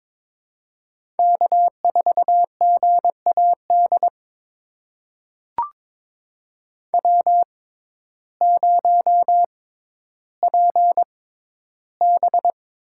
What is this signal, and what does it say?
Morse code, heard as a keyed tone, about 22 wpm, spaced as for 12 wpm, 700 Hz. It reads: K4GAD W0PB
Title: USA Call Signs - 6x Character Spacing - 22wpm